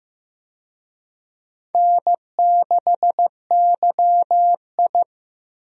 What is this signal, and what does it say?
Morse code, heard as a keyed tone, about 16 wpm, 700 Hz. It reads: N6YI
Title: USA Call Signs - 4 Characters - Repeated 2x - 15wpm